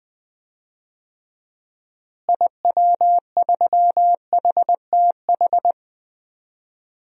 Morse code, heard as a keyed tone, about 20 wpm, 700 Hz. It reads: IW3HTH
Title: DXCC-100 Call Signs - 6 Characters - Repeated 3x - 20wpm